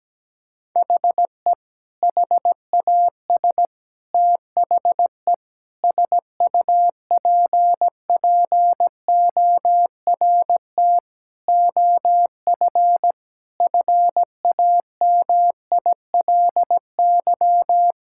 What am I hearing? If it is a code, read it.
Morse code, heard as a keyed tone, about 17 wpm, 700 Hz. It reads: HE HAS THE SUPPORT OF FAMILY
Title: Sentences from Top 1000 Words - Review Part 5 - 17wpm